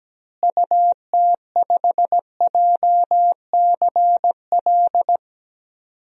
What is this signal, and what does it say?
Morse code, heard as a keyed tone, about 17 wpm, 700 Hz. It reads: UT5JCL
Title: DXCC-100 Call Signs - 6 Characters - Repeated 2x - 17wpm